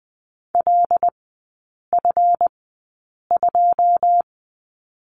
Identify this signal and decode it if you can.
Morse code, heard as a keyed tone, about 20 wpm, spaced as for 8 wpm, 700 Hz. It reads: LF2
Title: 152 Lesson 25e - Letter p - Three Characters 20wpm